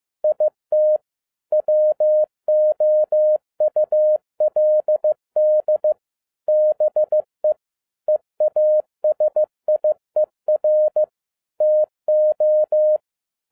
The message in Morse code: IT WOULD BE EASIER TO

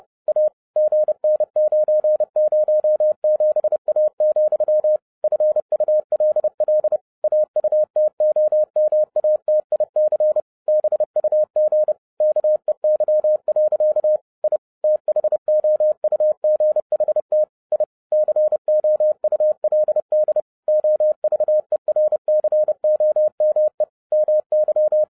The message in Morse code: A GN907A, FULL AUTOMATIC BUG KEY. I THOUGHT I COULD OVERCOME MY

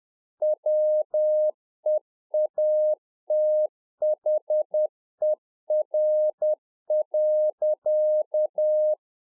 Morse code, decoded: WEATHER.

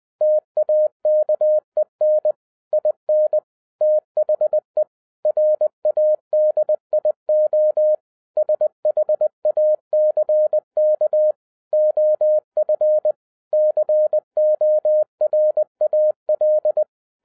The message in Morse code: TAKEN IN THE RADIO SHACK OF CORAL

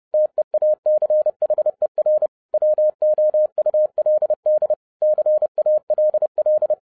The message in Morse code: TEACHER WOULD CALL

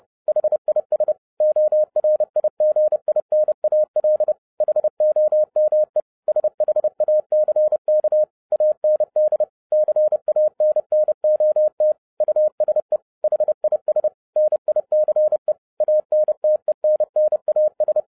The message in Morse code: HIS ORIGINAL HOME SHACK AND CANNOT USE HIS NICE ANTENNAS

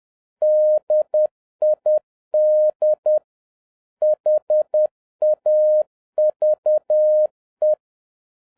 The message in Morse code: DID HAVE

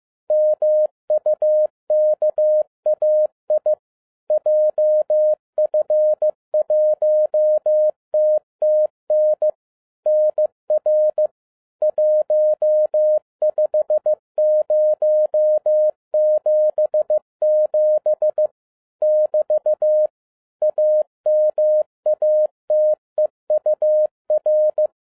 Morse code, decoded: MUKAI JF1TTN NR 15077 <BT> AMATEUR

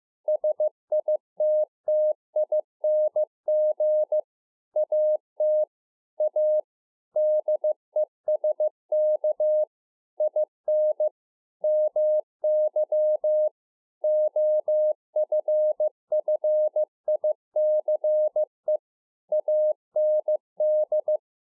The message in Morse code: SITTING AT A DESK IN MY OFFICE AND

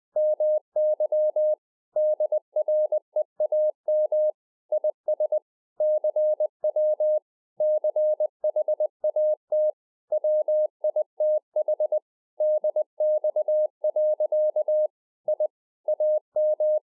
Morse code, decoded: MY DREAM IS CW CHAT WITH DX. I AM

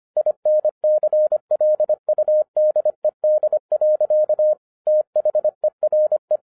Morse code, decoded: INCLUDED. THERE

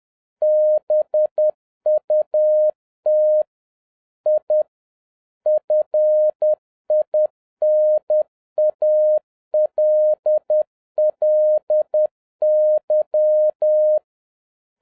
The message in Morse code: BUT I FINALLY

